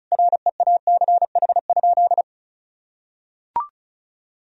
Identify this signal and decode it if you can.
Morse code, heard as a keyed tone, about 36 wpm, 700 Hz. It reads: REACH?
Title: Sentences from Top 1000 Words - Review Part 4 - 35wpm